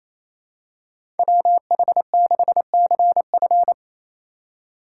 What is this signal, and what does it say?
Morse code, heard as a keyed tone, about 27 wpm, 700 Hz. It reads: WH6CF
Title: USA Call Signs - 2x Character Spacing - 28wpm